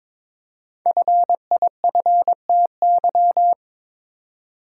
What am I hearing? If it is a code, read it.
Morse code, heard as a keyed tone, about 22 wpm, 700 Hz. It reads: FIFTY
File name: 5 Letter Words - First 3 Letters Given - 22wpm